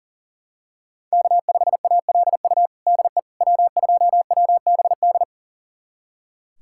Morse code, heard as a keyed tone, about 40 wpm, 700 Hz. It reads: K5ALU DE W2WBD
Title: USA Callsign DE Callsign - 5 Characters - 40wpm